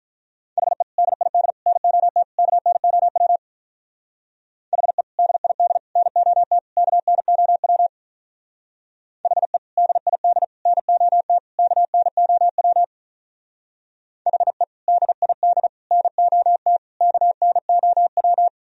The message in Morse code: HE DID NOT KNOW HE DID NOT KNOW HE DID NOT KNOW HE DID NOT KNOW

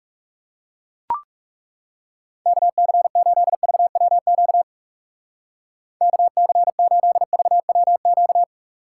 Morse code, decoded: KC8VWQ KC8VWQ

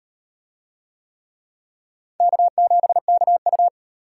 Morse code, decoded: K7KU

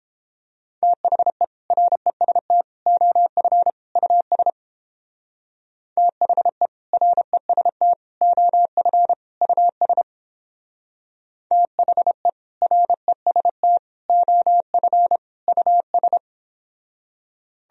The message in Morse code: THE REST OF US THE REST OF US THE REST OF US

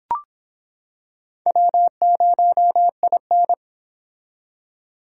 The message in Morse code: W0IN

